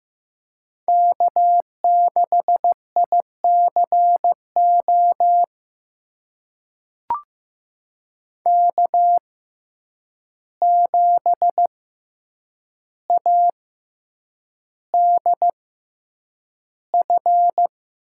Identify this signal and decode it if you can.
Morse code, heard as a keyed tone, about 15 wpm, spaced as for 8 wpm, 700 Hz. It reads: K6ICO K7ADF
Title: USA Call Signs - 5 Characters - 6x Character Spacing - 15wpm